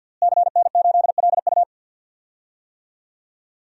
Morse code, heard as a keyed tone, about 53 wpm, 700 Hz. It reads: CN8LU
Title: DXCC-100 Call Signs - 5 Characters - Repeated 2x - 50wpm